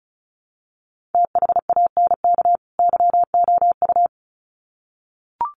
Morse code, heard as a keyed tone, about 34 wpm, 700 Hz. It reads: THANK YOU
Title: Sets of 2 Words 35wpm